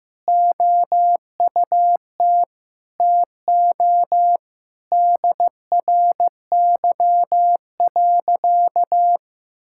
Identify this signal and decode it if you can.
Morse code, heard as a keyed tone, about 16 wpm, 700 Hz. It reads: OUT TO DRY.
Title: Sentences from Top 1600 Words 15wpm